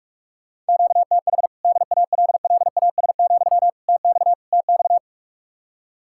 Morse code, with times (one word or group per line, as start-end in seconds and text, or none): 0.7-1.5 QTH
1.6-3.7 DALLAS,
3.9-4.3 TX
4.5-5.0 TX